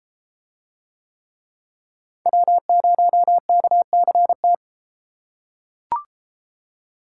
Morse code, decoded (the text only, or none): W0KCT